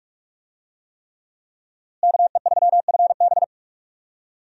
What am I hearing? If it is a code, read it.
Morse code, heard as a keyed tone, about 45 wpm, 700 Hz. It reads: KE3FB
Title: USA Call Signs - 5 Characters - Repeated 3x - 45wpm